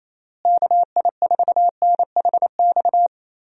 Morse code, heard as a keyed tone, about 28 wpm, 700 Hz. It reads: KI4NHX
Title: USA Call Signs - 6 Characters - Repeated 2x - 28wpm